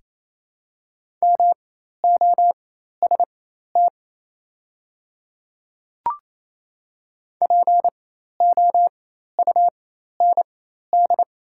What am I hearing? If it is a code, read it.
Morse code, heard as a keyed tone, about 28 wpm, spaced as for 13 wpm, 700 Hz. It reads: MOST POUND
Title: 4x Character Spacing - Top 500 Words - 28wpm